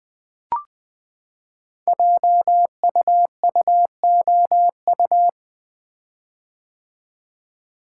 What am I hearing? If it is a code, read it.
Morse code, heard as a keyed tone, about 20 wpm, 700 Hz. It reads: JUUOU
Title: Code Groups - 5 Letters - 20wpm